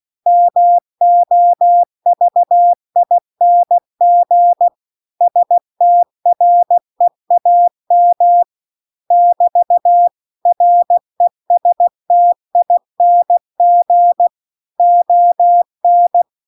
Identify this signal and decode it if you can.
Morse code, heard as a keyed tone, about 16 wpm, 700 Hz. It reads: MOVING STREAM <BT> RESTING ON
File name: CW-The-Big-Twoheaded-River-2-16WPM